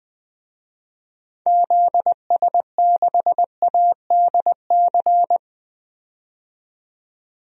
Morse code, heard as a keyed tone, about 20 wpm, 700 Hz. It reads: ZS6ADC
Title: DXCC-100 Call Signs - 6 Characters - 20wpm